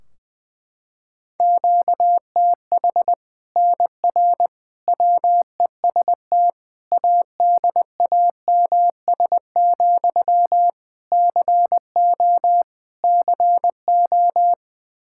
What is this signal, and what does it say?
Morse code, heard as a keyed tone, about 20 wpm, 700 Hz. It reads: QTH NR WEST ADAMS, CO CO
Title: Practice QSO - Exchange 1 - Beginning - 20wpm